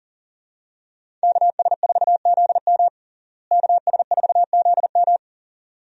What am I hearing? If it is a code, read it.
Morse code, heard as a keyed tone, about 40 wpm, 700 Hz. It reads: KS4ZM KS4ZM
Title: USA Call Signs - 5 Characters - Repeated 2x - 40wpm